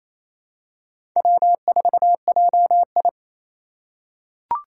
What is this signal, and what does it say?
Morse code, heard as a keyed tone, about 27 wpm, 700 Hz. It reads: W4JI
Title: USA Call Signs - 4 Characters - 4x Character Spacing - 28wpm